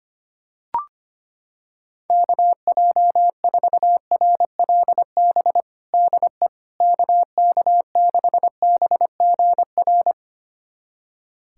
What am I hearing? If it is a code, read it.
Morse code, heard as a keyed tone, about 25 wpm, 700 Hz. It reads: KJ4RLB DE KK6BGR
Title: USA Callsign DE Callsign - 6 Characters - 25wpm